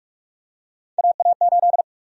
Morse code, heard as a keyed tone, about 41 wpm, 700 Hz. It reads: AA8
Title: USA Call Sign Prefixes - Repeated 2x - 45wpm